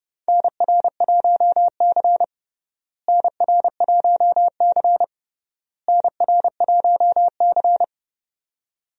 Morse code, heard as a keyed tone, about 30 wpm, 700 Hz. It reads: NR1C NR1C NR1C
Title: USA Call Signs - 4 Characters - Repeated 3x - 30wpm